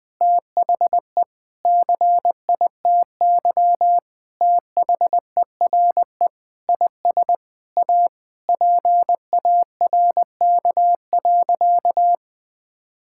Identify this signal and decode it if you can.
Morse code, heard as a keyed tone, about 19 wpm, 700 Hz. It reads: THE CITY THERE IS A PARK.
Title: Sentences with 5-Word Phrases - Spoken Beginning - 20wpm